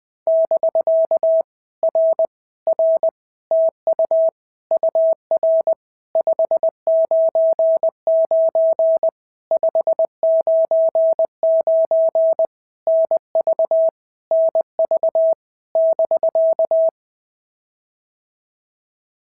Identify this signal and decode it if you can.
Morse code, heard as a keyed tone, about 20 wpm, 650 Hz. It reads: <BK> R R TU UR 599 599 NV NV <BK>
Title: POTA QSO Practice - 20wpm